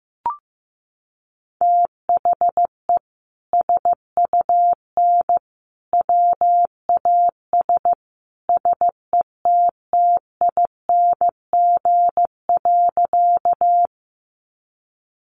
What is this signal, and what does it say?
Morse code, heard as a keyed tone, about 15 wpm, 700 Hz. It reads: THE SUN WAS SETTING.
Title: Sentences from Top 1300 Words 15wpm